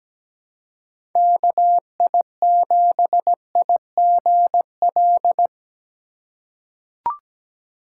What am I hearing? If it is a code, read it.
Morse code, heard as a keyed tone, about 17 wpm, 700 Hz. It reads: KI7IGL
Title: USA Call Signs - 6 Characters - 3x Character Spacing - 17wpm